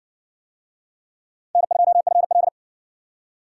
Morse code, heard as a keyed tone, about 56 wpm, 700 Hz. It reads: N2FL